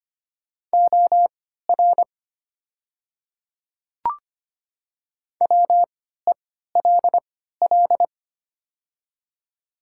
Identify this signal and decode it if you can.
Morse code, heard as a keyed tone, about 25 wpm, spaced as for 14 wpm, 700 Hz. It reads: OR WELL
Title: 3x Character Spacing - Top 200 Words - 25wpm